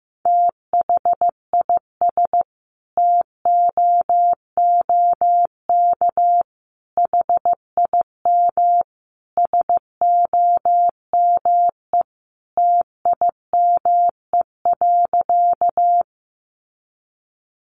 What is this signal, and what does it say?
Morse code, heard as a keyed tone, about 15 wpm, 700 Hz. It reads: THIS TOOK HIM SOME TIME.
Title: Sentences from Top 300 Words 15wpm